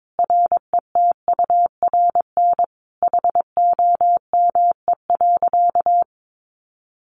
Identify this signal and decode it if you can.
Morse code, heard as a keyed tone, about 22 wpm, 700 Hz. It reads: RETURN HOME.
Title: Sentences from Top 1400 Words 22wpm